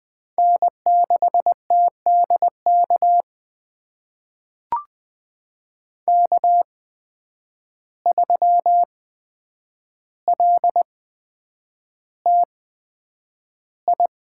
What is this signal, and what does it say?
Morse code, heard as a keyed tone, about 20 wpm, spaced as for 9 wpm, 700 Hz. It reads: N6TDK K3LTI
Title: USA Call Signs - 5 Characters - 8x Character Spacing - 20wpm